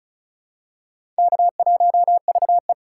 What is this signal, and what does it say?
Morse code, heard as a keyed tone, about 34 wpm, 700 Hz. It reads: K1VE
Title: USA Call Signs - 4 Characters - 3x Character Spacing - 35wpm